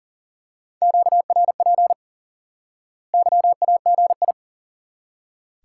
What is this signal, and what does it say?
Morse code, heard as a keyed tone, about 40 wpm, 700 Hz. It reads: QRP YAGI
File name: IQE Course 3 - Lesson 89 - YAGI - Two Elements - 40wpm